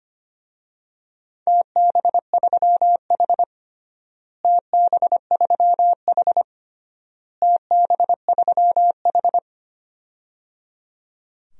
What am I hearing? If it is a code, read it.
Morse code, heard as a keyed tone, about 25 wpm, 700 Hz. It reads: TB3H TB3H TB3H
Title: DXCC-100 Call Signs - 4 Characters - Repeated 3x - 25wpm